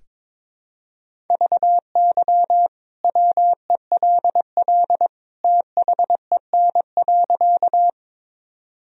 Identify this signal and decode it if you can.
Morse code, heard as a keyed tone, about 21 wpm, 700 Hz. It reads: VY WELL THEN.
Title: Sentences from Top 200 Words and Abbreviations 22wpm